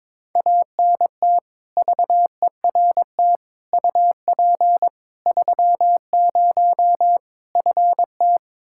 Morse code, heard as a keyed tone, about 22 wpm, 700 Hz. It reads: ANT VERT UP 30 FT
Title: K1USN SST QSO Practice 22wpm